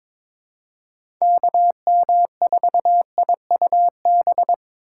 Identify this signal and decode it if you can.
Morse code, heard as a keyed tone, about 22 wpm, 700 Hz. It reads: KM4IUB
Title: USA Call Signs - 6 Characters - 8x Character Spacing - 22wpm